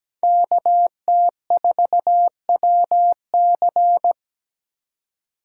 Morse code, heard as a keyed tone, about 20 wpm, 700 Hz. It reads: KT4WC
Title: Speed Racing - USA 2x2 Call Signs 17wpm